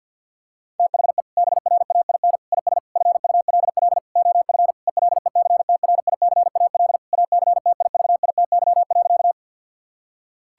Morse code, heard as a keyed tone, about 50 wpm, 700 Hz. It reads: THE BRAIN IS FULL OF ELECTRICAL ACTIVITY.
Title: Sentences from Top 1000 Words - Review Part 6 - 50wpm